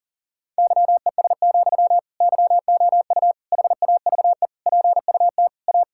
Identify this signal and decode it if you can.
Morse code, heard as a keyed tone, about 40 wpm, 700 Hz. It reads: YES, YOU HAVE PUT A
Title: Sentences from Top 1600 Words 40wpm